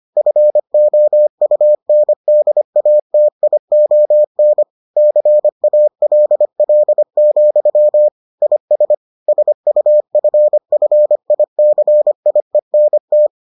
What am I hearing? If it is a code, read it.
Morse code, heard as a keyed tone, about 25 wpm, 600 Hz. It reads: FOUNDATION CALL, IS SUFFICIENT